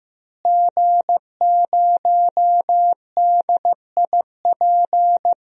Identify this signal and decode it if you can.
Morse code, heard as a keyed tone, about 16 wpm, 700 Hz. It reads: G0DIP
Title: DXCC-100 Call Signs - 5 Characters - Repeated 2x - 15wpm